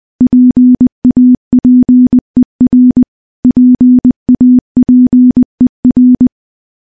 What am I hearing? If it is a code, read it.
Morse code, heard as a keyed tone, about 21 wpm, 250 Hz. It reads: PAPER PAPER
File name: morse-training-20WPM